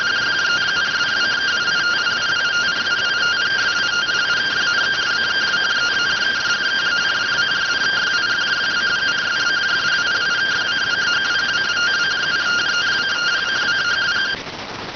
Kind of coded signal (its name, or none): two-tone teleprinter signal